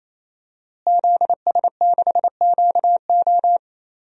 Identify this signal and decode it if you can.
Morse code, heard as a keyed tone, about 27 wpm, 700 Hz. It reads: ZS6QO
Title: DXCC-100 Call Signs - 5 Characters - Repeated 3x - 28wpm